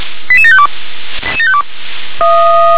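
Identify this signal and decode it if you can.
Five-tone selective call (ZVEI), heard as quick runs of five tones, tones of about 70 ms. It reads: 88752
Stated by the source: ZVEI five-tone call received on SDR.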